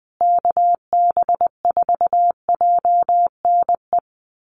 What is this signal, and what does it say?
Morse code, heard as a keyed tone, about 20 wpm, 700 Hz. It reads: KB4JNE
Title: USA Call Signs Repeated 3x 20wpm